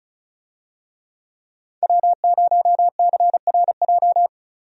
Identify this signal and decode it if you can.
Morse code, heard as a keyed tone, about 36 wpm, 700 Hz. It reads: W0CRJ
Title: USA Call Signs - 5 Characters - 35wpm